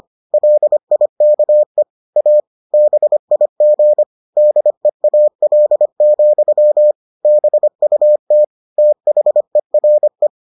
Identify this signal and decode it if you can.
Morse code, heard as a keyed tone, about 25 wpm, 600 Hz. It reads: LIKE A BIG DEAL, BUT THERE